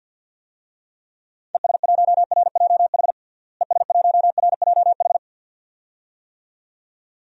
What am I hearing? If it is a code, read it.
Morse code, heard as a keyed tone, about 50 wpm, 700 Hz. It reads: ES1RJH ES1RJH
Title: DXCC-100 Call Signs - 6 Characters - Repeated 2x - 50wpm